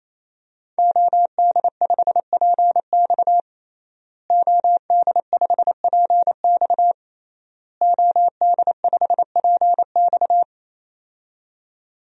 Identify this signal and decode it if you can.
Morse code, heard as a keyed tone, about 28 wpm, 700 Hz. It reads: OD5PX OD5PX OD5PX
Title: DXCC-100 Call Signs - 5 Characters - Repeated 3x - 28wpm